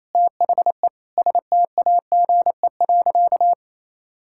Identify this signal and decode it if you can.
Morse code, heard as a keyed tone, about 28 wpm, 700 Hz. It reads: THE STAGE.
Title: Sentences from Top 1600 Words 28wpm